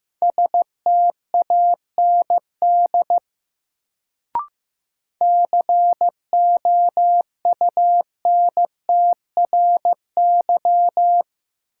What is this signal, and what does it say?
Morse code, heard as a keyed tone, about 15 wpm, 700 Hz. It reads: STAND COUNTRY
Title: Sentences from Top 300 Words - New Word Drill - 15wpm